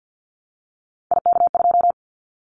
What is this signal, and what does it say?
Morse code, heard as a keyed tone, about 50 wpm, 700 Hz. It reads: IK2
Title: Callsign Prefixes for Top 50 DXCC - 50wpm